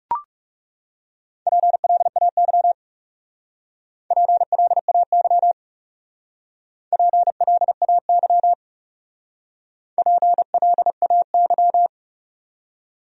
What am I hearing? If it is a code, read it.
Morse code, heard as a keyed tone, about 37 wpm, 700 Hz. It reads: PLAY PLAY PLAY PLAY
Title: Speed-Racing Top 500 Words 30wpm